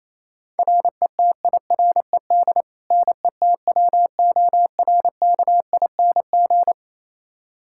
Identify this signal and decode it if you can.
Morse code, heard as a keyed tone, about 28 wpm, 700 Hz. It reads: RETIRED NETWORKING
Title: K1USN SST QSO Practice 28wpm